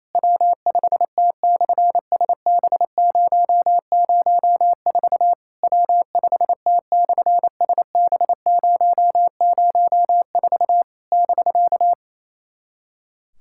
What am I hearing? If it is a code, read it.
Morse code, heard as a keyed tone, about 28 wpm, 700 Hz. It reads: W5T/SB004 W5T/SB004 <BK>
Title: SOTA QSO Practice 28wpm